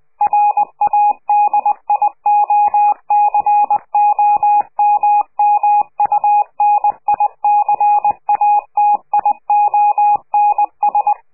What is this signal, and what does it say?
Morse code, heard as a keyed tone, about 20 wpm, 830 Hz. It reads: RADIOCOMMUNICATIONS
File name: CQ de PCH laatste bericht farewell